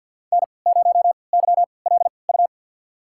Morse code, heard as a keyed tone, about 53 wpm, spaced as for 29 wpm, 700 Hz. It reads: N0YLU
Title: USA Call Signs - 5 Characters - 3x Character Spacing - 50wpm